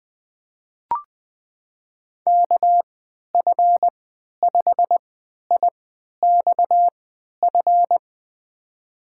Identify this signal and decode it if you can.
Morse code, heard as a keyed tone, about 20 wpm, spaced as for 11 wpm, 700 Hz. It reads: KF5IXF
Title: USA Call Signs - 6 Characters - 3x Character Spacing - 20wpm